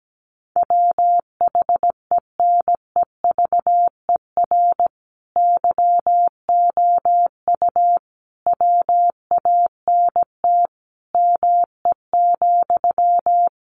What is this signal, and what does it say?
Morse code, heard as a keyed tone, about 17 wpm, 700 Hz. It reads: WHENEVER YOU WANT ME,
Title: Sentences from Top 1200 Words 17wpm